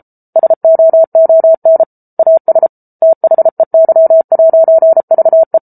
Morse code, heard as a keyed tone, about 34 wpm, 650 Hz. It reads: SOON AS THEY'VE
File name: Book Two / 07-Chapter 7 - 33 WPM